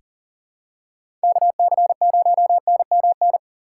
Morse code, heard as a keyed tone, about 40 wpm, 700 Hz. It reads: KC0NMN
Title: USA Call Signs - 6 Characters - 3x Character Spacing - 40wpm